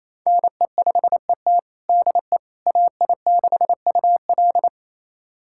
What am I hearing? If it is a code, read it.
Morse code, heard as a keyed tone, about 28 wpm, 700 Hz. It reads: NE5ET DE AI6UL
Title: USA Callsign DE Callsign - 5 Characters - 28wpm